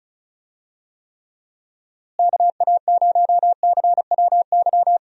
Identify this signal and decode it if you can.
Morse code, heard as a keyed tone, about 36 wpm, 700 Hz. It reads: KA0CWY
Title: USA Call Signs - 6 Characters - Repeated 3x - 35wpm